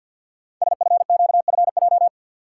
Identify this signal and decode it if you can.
Morse code, heard as a keyed tone, about 48 wpm, 700 Hz. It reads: SFQFJ
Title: Code Groups - 5 Letters - 50wpm